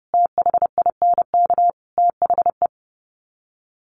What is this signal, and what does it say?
Morse code, heard as a keyed tone, about 30 wpm, 700 Hz. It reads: THINK THE
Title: Sets of 5 Words Encore 30wpm